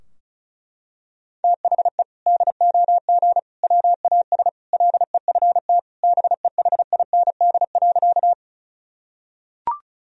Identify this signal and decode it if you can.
Morse code, heard as a keyed tone, about 34 wpm, 700 Hz. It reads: THE DOG WAS LEFT BEHIND.
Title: Sentences from Top 1000 Words - Review Part 2 - 35wpm